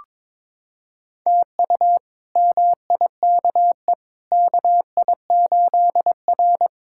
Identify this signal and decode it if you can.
Morse code, heard as a keyed tone, about 22 wpm, 700 Hz. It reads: TU MIKE KI8R
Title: K1USN SST QSO Practice - 22wpm